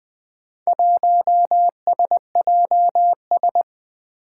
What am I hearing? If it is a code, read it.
Morse code, heard as a keyed tone, about 20 wpm, 700 Hz. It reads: 1SJS
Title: Four Challenging Letter-Number Combinations 20wpm